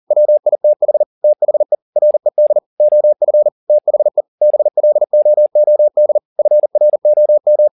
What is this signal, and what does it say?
Morse code, heard as a keyed tone, about 40 wpm, 600 Hz. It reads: WITH THE RED OF THE BLOOD FROM
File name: the old man and the sea 2 40wpm